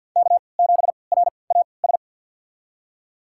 Morse code, heard as a keyed tone, about 52 wpm, spaced as for 29 wpm, 700 Hz. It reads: K7RAS
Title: USA Call Signs - 5 Characters - 3x Character Spacing - 50wpm